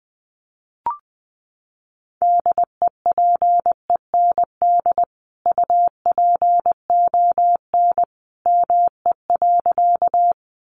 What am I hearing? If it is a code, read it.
Morse code, heard as a keyed tone, about 20 wpm, 700 Hz. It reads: DEPEND UPON ME.
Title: Sentences from Top 1300 Words 20wpm